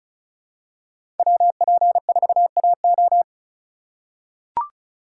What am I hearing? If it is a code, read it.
Morse code, heard as a keyed tone, about 36 wpm, 700 Hz. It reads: WP4AO
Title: USA Call Signs - 8x Character Spacing - 35wpm